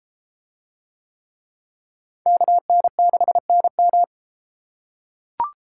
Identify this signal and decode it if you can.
Morse code, heard as a keyed tone, about 32 wpm, 700 Hz. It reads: KN6NM